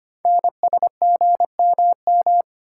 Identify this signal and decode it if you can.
Morse code, heard as a keyed tone, about 25 wpm, 700 Hz. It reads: NSGMM
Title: Code Groups - 5 Letters - 25wpm